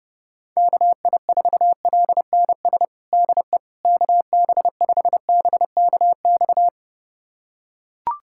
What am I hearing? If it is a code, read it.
Morse code, heard as a keyed tone, about 30 wpm, 700 Hz. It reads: KI4LNS DE KB5BKX
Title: USA Callsign DE Callsign 30wpm